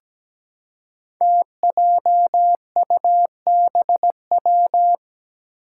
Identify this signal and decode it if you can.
Morse code, heard as a keyed tone, about 17 wpm, 700 Hz. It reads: TJUBW
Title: Five Challenging Letter-Number Combinations 17wpm